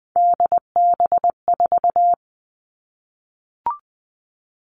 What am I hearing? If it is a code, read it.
Morse code, heard as a keyed tone, about 20 wpm, 700 Hz. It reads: DB4
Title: All Cut Numbers - 3-Digit 20wpm